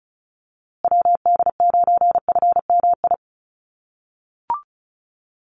Morse code, heard as a keyed tone, about 34 wpm, 700 Hz. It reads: WD9FMI
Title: USA Call Signs Repeated 3x 35wpm